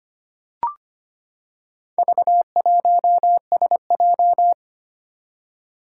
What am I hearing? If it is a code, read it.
Morse code, heard as a keyed tone, about 25 wpm, 700 Hz. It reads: V1SJ
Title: Four Challenging Letter-Number Combinations 25wpm